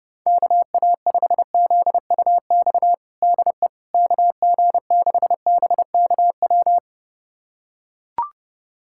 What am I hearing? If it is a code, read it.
Morse code, heard as a keyed tone, about 32 wpm, 700 Hz. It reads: KA5ZUX DE KG6BKW
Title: USA Callsign DE Callsign 30wpm